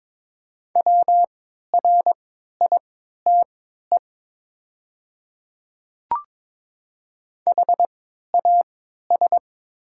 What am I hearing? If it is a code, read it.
Morse code, heard as a keyed tone, about 22 wpm, spaced as for 12 wpm, 700 Hz. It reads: WRITE HAS
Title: 3x Character Spacing - Top 100 Words - 22wpm